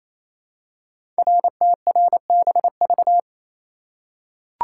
Morse code, heard as a keyed tone, about 28 wpm, 700 Hz. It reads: RTRBV
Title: Code Groups - 5 Letters - 28wpm